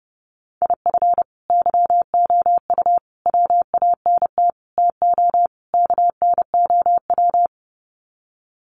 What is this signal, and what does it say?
Morse code, heard as a keyed tone, about 30 wpm, 700 Hz. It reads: IF YOU WANT TO KNOW
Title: Sets of 5 Words Encore2 30wpm